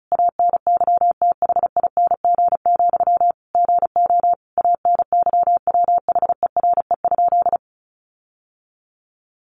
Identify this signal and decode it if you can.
Morse code, heard as a keyed tone, about 34 wpm, 700 Hz. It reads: ANYTHING, GO ANYWHERE?
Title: Sentences from Top 1400 Words 35wpm